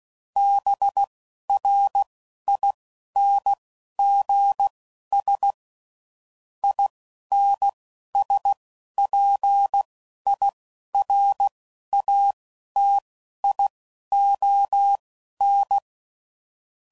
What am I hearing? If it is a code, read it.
Morse code, heard as a keyed tone, about 16 wpm, spaced as for 11 wpm, 800 Hz. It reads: BRINGS INSPIRATION
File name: Sayings / Quotes / Persperation